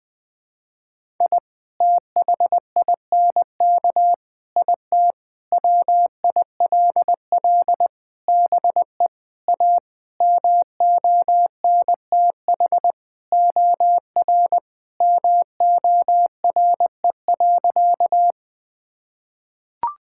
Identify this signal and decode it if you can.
Morse code, heard as a keyed tone, about 21 wpm, 700 Hz. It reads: I THINK IT WILL BE A MONTH OR MORE.
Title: Sentences from Top 1000 Words - Review Part 5 - 20wpm